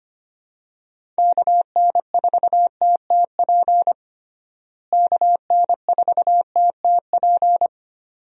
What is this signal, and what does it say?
Morse code, heard as a keyed tone, about 25 wpm, 700 Hz. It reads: KN4TTP KN4TTP
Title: USA Call Signs - 6 Characters - Repeated 2x - 25wpm